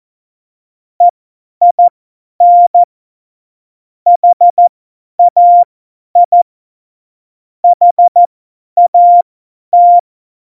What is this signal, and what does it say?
Morse code, heard as a keyed tone, about 14 wpm, spaced as for 10 wpm, 700 Hz. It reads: EIN HAI HAT
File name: Morsekurs / Witze / Witz 26. Juni 2018, 50 F70